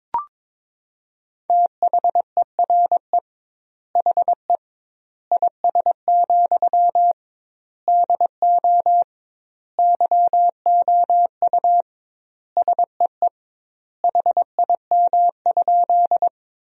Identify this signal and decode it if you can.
Morse code, heard as a keyed tone, about 21 wpm, 700 Hz. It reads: THERE HE IS, DO YOU SEE HIM?
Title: 2x Word-Spacing - Sentences from Top 100 Words - 22wpm